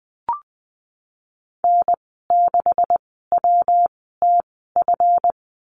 Morse code, heard as a keyed tone, about 20 wpm, spaced as for 15 wpm, 700 Hz. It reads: N6WTF
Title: 140 Lesson 23g - Letter f - Call Signs 20wpm